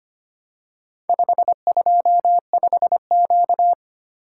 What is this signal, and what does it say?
Morse code, heard as a keyed tone, about 25 wpm, 700 Hz. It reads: 525Q
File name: practice / 25wpm / Four Letter-Number Combinations - 25wpm